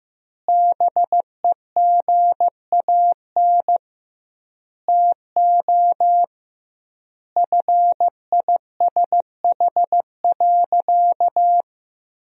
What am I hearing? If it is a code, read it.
Morse code, heard as a keyed tone, about 16 wpm, 700 Hz. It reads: BEGAN TO FISH.
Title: 2x Word-Spacing - Sentences from Top 300 Words - 15wpm